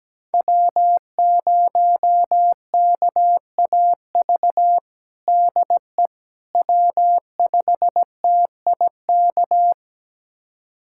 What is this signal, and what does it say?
Morse code, heard as a keyed tone, about 17 wpm, 700 Hz. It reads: W0KAV DE W5TIK
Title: USA Callsign DE Callsign - 5 Characters - 17wpm